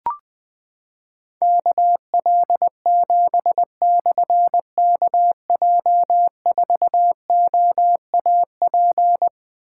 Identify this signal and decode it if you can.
Morse code, heard as a keyed tone, about 20 wpm, 700 Hz. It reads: KL7/KJ4OAP
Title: Difficult Call Signs 20wpm